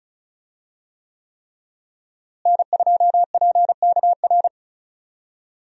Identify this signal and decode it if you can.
Morse code, heard as a keyed tone, about 35 wpm, 700 Hz. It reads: N2PKR
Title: USA Call Signs - 5 Characters - Repeated 3x - 35wpm